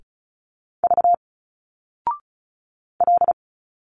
Morse code, heard as a keyed tone, about 36 wpm, 700 Hz. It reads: V L
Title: Single Letters 35wpm